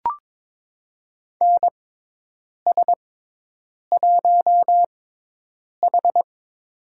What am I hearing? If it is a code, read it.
Morse code, heard as a keyed tone, about 22 wpm, spaced as for 8 wpm, 700 Hz. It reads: NS1H
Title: USA Call Signs - 4 Characters - 6x Character Spacing - 22wpm